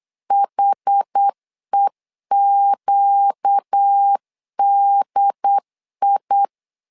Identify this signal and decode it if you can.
Morse code, heard as a keyed tone, about 8 wpm, 800 Hz. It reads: HEQDI
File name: 4 Q 8wpm